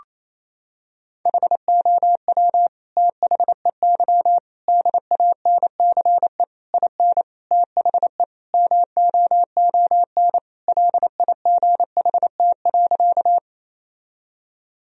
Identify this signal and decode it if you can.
Morse code, heard as a keyed tone, about 28 wpm, 700 Hz. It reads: HOW THEY DANCE IN THE MOON LIGHT.
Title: Sentences from Top 1000 Words - Review Part 4 - 28wpm